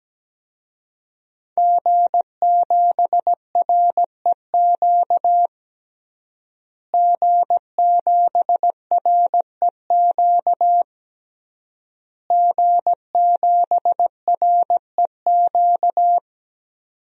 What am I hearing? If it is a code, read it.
Morse code, heard as a keyed tone, about 17 wpm, 700 Hz. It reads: G7REQ G7REQ G7REQ
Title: DXCC-100 Call Signs - 5 Characters - Repeated 3x - 17wpm